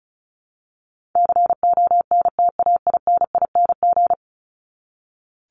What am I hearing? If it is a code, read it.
Morse code, heard as a keyed tone, about 34 wpm, 700 Hz. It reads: CONTAINING
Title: ING Suffix - Top 100 Words with Base Words 35wpm